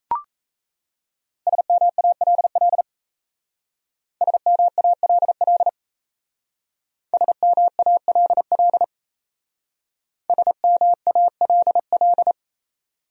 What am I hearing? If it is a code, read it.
Morse code, heard as a keyed tone, about 35 wpm, 700 Hz. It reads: SMALL SMALL SMALL SMALL
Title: Speed-Racing Top 500 Words 28wpm